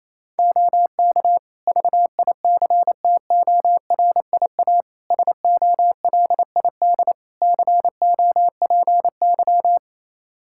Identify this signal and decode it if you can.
Morse code, heard as a keyed tone, about 27 wpm, 700 Hz. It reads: OK VICTORIA SOLID COPY
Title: Practice QSO - Exchange 3 - More Info - 28wpm